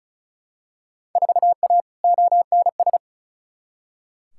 Morse code, heard as a keyed tone, about 36 wpm, 700 Hz. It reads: 4A ONS
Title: Field Day Exchanges - 35wpm